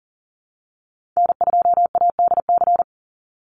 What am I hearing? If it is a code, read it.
Morse code, heard as a keyed tone, about 40 wpm, 700 Hz. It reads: N2ADC
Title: Call Signs Repeated 3x - 40wpm